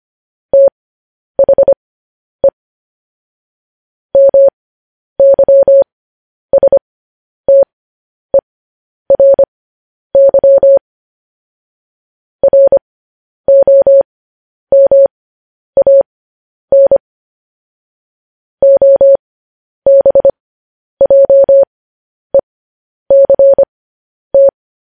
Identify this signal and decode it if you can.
Morse code, heard as a keyed tone, about 25 wpm, spaced as for 10 wpm, 550 Hz. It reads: THE MYSTERY ROMAN OBJECT
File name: podcast / News headlines-1x-25x10